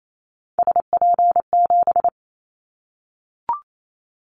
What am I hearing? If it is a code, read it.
Morse code, heard as a keyed tone, about 28 wpm, 700 Hz. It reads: SP7
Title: Callsign Prefixes for Top 25 DXCC - 28wpm